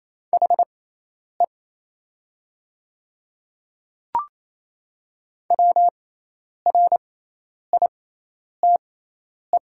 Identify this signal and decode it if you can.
Morse code, heard as a keyed tone, about 28 wpm, spaced as for 10 wpm, 700 Hz. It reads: HE WRITE
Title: 6x Character Spacing - Top 200 Words - 28wpm